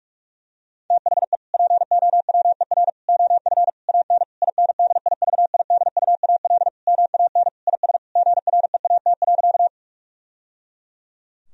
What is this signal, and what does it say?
Morse code, heard as a keyed tone, about 46 wpm, 700 Hz. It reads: THE POWER OF AN INDIVIDUAL MAN IS GREAT.
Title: Sentences from Top 1000 Words - Review Part 4 - 45wpm